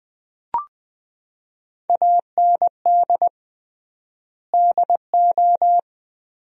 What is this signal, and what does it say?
Morse code, heard as a keyed tone, about 20 wpm, 700 Hz. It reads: AND DO
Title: 3x Word-Spacing - Sentences from Top 200 Words - 20wpm